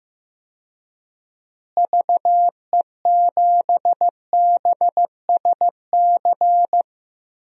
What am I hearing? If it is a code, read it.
Morse code, heard as a keyed tone, about 15 wpm, 700 Hz. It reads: VE7BSC
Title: DXCC-100 Call Signs - 6 Characters - Repeated 2x - 15wpm